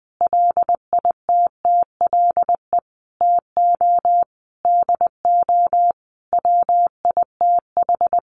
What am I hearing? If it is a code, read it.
Morse code, heard as a keyed tone, about 19 wpm, 700 Hz. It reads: LITTLE TO DO WITH
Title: Sentences from Top 1500 Words 20wpm